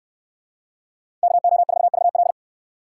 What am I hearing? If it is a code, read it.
Morse code, heard as a keyed tone, about 69 wpm, 700 Hz. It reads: KO4VL